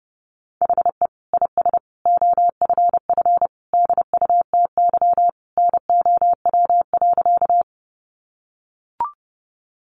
Morse code, heard as a keyed tone, about 30 wpm, 700 Hz. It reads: HE IS OFF DUTY NOW.
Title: Sentences from Top 1500 Words 30wpm